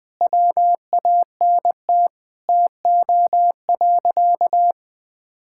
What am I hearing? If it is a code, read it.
Morse code, heard as a keyed tone, about 20 wpm, 700 Hz. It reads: WANT TO.
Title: Sentences from Top 1600 Words 20wpm